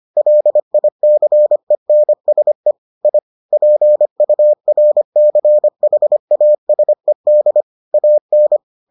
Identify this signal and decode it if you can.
Morse code, heard as a keyed tone, about 25 wpm, 600 Hz. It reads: LICENSE I PURCHASED AN